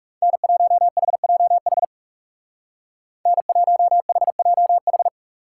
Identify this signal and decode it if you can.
Morse code, heard as a keyed tone, about 37 wpm, 700 Hz. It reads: N1HJH N1HJH
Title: Speed Racing - USA 1x3 Call Signs 30wpm